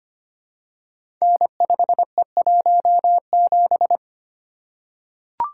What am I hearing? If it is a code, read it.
Morse code, heard as a keyed tone, about 25 wpm, 700 Hz. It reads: N5E17
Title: Code Groups - 5 Letter:Numbers - 25wpm